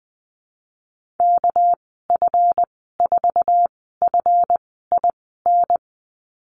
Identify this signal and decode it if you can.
Morse code, heard as a keyed tone, about 20 wpm, spaced as for 15 wpm, 700 Hz. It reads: KF4FIN
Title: 231 Lesson 37g - Letter x - Call Signs 20wpm